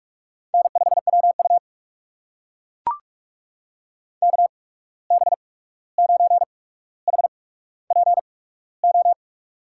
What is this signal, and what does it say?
Morse code, heard as a keyed tone, about 44 wpm, spaced as for 17 wpm, 700 Hz. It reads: N5WU KB9HPO
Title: USA Call Signs - 8x Character Spacing - 45wpm